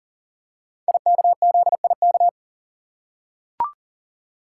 Morse code, heard as a keyed tone, about 39 wpm, 700 Hz. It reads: IKZIK
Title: Code Groups - 5 Letters - 40wpm